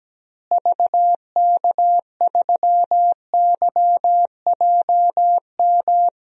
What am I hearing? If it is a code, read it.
Morse code, heard as a keyed tone, about 17 wpm, 700 Hz. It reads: VK3YJM
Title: DXCC-100 Call Signs - 6 Characters - Repeated 2x - 17wpm